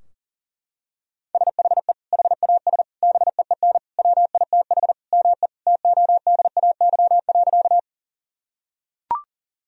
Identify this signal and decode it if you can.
Morse code, heard as a keyed tone, about 40 wpm, 700 Hz. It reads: SHE HAS BEEN WITH ME TODAY.
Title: Sentences from Top 1000 Words - Review Part 4 - 40wpm